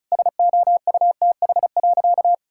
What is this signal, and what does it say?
Morse code, heard as a keyed tone, about 36 wpm, 700 Hz. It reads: SOUTH.
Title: Sentences from Top 1000 Words - Review Part 6 - 35wpm